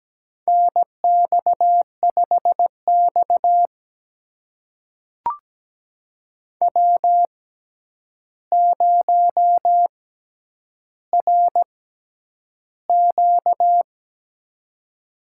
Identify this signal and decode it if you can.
Morse code, heard as a keyed tone, about 17 wpm, spaced as for 9 wpm, 700 Hz. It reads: NX5X W0RQ
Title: USA Call Signs - 4 Characters - 6x Character Spacing - 17wpm